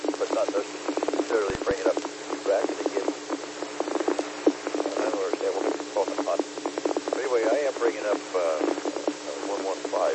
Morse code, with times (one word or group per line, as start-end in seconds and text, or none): none